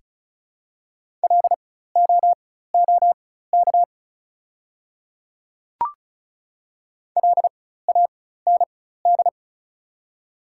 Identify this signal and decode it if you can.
Morse code, heard as a keyed tone, about 36 wpm, spaced as for 16 wpm, 700 Hz. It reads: LOOK LAND
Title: 4x Character Spacing - Top 200 Words - 35wpm